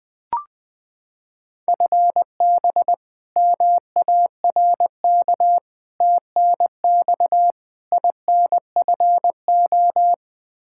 Morse code, heard as a keyed tone, about 21 wpm, 700 Hz. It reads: FB MARK TNX INFO
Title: K1USN SST QSO Practice 20wpm